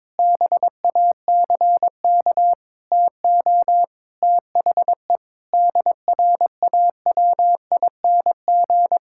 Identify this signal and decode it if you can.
Morse code, heard as a keyed tone, about 22 wpm, 700 Hz. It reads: BACK TO THE DRAWING
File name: Sentences with 7-Word Phrases - 22wpm